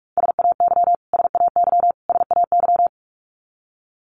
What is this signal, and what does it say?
Morse code, heard as a keyed tone, about 45 wpm, 700 Hz. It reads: SAY SAY SAY
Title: Top 500 Words Repeated 45wpm